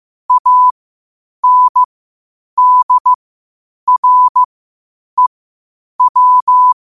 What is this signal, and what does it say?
Morse code, heard as a keyed tone, about 15 wpm, spaced as for 8 wpm, 1000 Hz. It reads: ANDREW
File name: qotdfiles / 8wpm